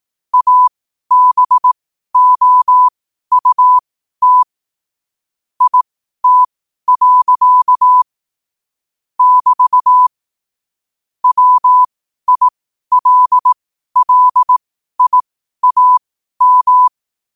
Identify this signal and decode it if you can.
Morse code, heard as a keyed tone, about 18 wpm, spaced as for 12 wpm, 1000 Hz. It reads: ABOUT IT. <BT> WILLIAM
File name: qotdfiles / 12wpm